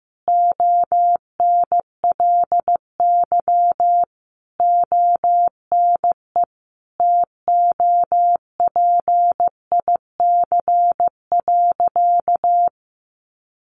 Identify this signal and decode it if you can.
Morse code, heard as a keyed tone, about 15 wpm, 700 Hz. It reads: ONLY ONE TOPIC.
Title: Sentences from Top 1200 Words 15wpm